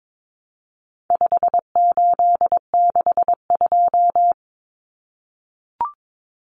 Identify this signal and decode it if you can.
Morse code, heard as a keyed tone, about 22 wpm, 700 Hz. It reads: 5862
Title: Common Cut Numbers - 4-Digit 22wpm